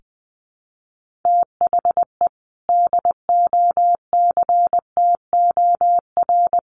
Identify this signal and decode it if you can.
Morse code, heard as a keyed tone, about 20 wpm, 700 Hz. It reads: THE DOCTOR
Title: Sentences from Top 1500 Words 20wpm